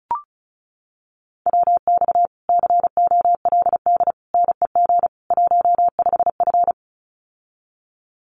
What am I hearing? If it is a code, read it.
Morse code, heard as a keyed tone, about 34 wpm, 700 Hz. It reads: WX COLD NEG 15F
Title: Common QSO Phrases Encore 35wpm